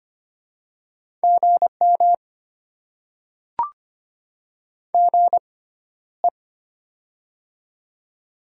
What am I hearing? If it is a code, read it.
Morse code, heard as a keyed tone, about 25 wpm, spaced as for 13 wpm, 700 Hz. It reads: GM GE
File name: Top 20 QSO Elements - 6x Character Spacing - 25wpm